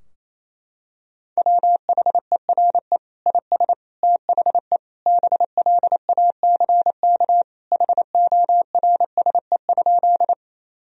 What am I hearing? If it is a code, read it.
Morse code, heard as a keyed tone, about 28 wpm, 700 Hz. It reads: WHERE IS THE BLACK HORSE?
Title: Sentences from Top 1000 Words - Review Part 2 - 28wpm